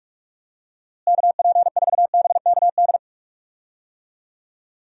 Morse code, heard as a keyed tone, about 41 wpm, 700 Hz. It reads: KP4BKD
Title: DXCC-100 Call Signs - 6 Characters - Repeated 3x - 45wpm